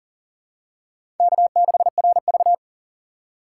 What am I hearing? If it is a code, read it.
Morse code, heard as a keyed tone, about 40 wpm, 700 Hz. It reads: K6RV
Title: USA Call Signs - 4 Characters - 4x Character Spacing - 40wpm